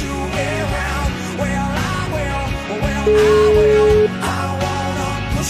Ringback tone (Europe, 425 Hz), with noise.